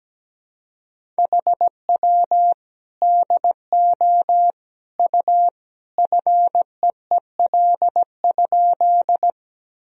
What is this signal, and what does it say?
Morse code, heard as a keyed tone, about 17 wpm, 700 Hz. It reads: HW DO U FEEL?
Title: Sentences from Top 400 Words and Abbreviations 17wpm